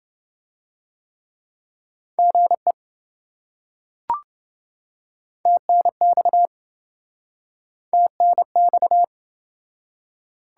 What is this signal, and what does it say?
Morse code, heard as a keyed tone, about 26 wpm, 700 Hz. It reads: GE TNX TNX